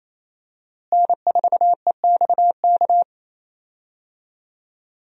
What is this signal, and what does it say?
Morse code, heard as a keyed tone, about 28 wpm, 700 Hz. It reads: N4EXK
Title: USA Call Signs - 5 Characters - Repeated 2x - 28wpm